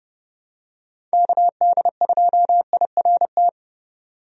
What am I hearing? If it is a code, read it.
Morse code, heard as a keyed tone, about 30 wpm, 700 Hz. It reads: KD2IRT
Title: USA Call Signs - 6 Characters - 30wpm